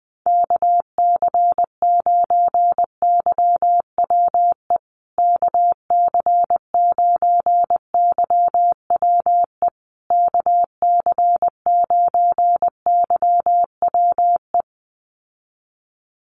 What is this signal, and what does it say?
Morse code, heard as a keyed tone, about 19 wpm, 700 Hz. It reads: KC9YWE KC9YWE KC9YWE
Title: USA Call Signs Repeated 3x 20wpm